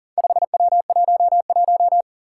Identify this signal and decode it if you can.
Morse code, heard as a keyed tone, about 37 wpm, 700 Hz. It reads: 5W11
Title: Four Challenging Letter-Number Combinations 40wpm